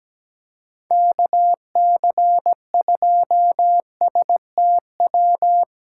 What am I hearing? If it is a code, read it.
Morse code, heard as a keyed tone, about 17 wpm, 700 Hz. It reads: KC2STW
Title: USA Callsign DE Callsign - 6 Characters - 17wpm